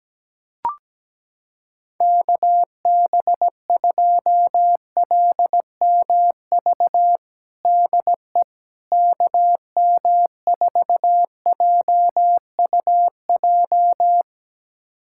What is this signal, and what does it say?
Morse code, heard as a keyed tone, about 17 wpm, 700 Hz. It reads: KB2LMV DE KM4JUJ
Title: USA Callsign DE Callsign - 6 Characters - 17wpm